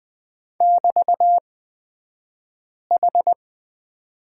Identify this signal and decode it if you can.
Morse code, heard as a keyed tone, about 20 wpm, spaced as for 5 wpm, 700 Hz. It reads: <BT>H
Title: 223 Lesson 36d - BT - Two Characters 20wpm